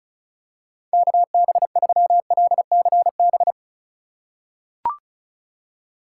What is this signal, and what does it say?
Morse code, heard as a keyed tone, about 36 wpm, 700 Hz. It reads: KB3LCB
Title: USA Call Signs - 6 Characters - 4x Character Spacing - 35wpm